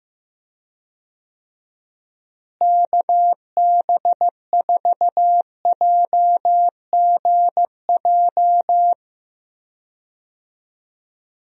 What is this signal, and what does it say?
Morse code, heard as a keyed tone, about 15 wpm, 700 Hz. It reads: KB4JGJ
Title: USA Call Signs - 6 Characters - Repeated 3x - 15wpm